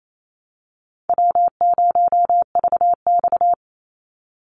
Morse code, heard as a keyed tone, about 28 wpm, 700 Hz. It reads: W0VX
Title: Call Signs - 28wpm